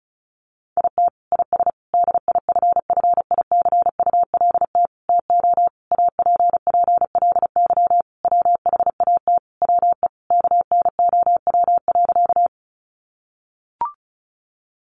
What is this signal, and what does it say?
Morse code, heard as a keyed tone, about 34 wpm, 700 Hz. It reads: IT IS DIFFICULT TO APPLY WHAT WE KNOW.
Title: Sentences from Top 1200 Words 35wpm